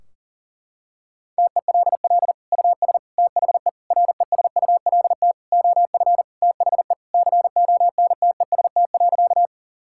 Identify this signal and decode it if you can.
Morse code, heard as a keyed tone, about 40 wpm, 700 Hz. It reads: TELL US THE RESULT OF THE CONTEST.
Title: Sentences from Top 1600 Words 40wpm